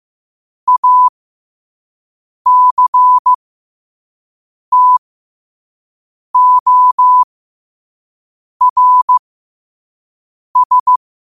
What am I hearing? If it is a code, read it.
Morse code, heard as a keyed tone, about 15 wpm, spaced as for 5 wpm, 1000 Hz. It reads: ACTORS
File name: qotdfiles / 5wpm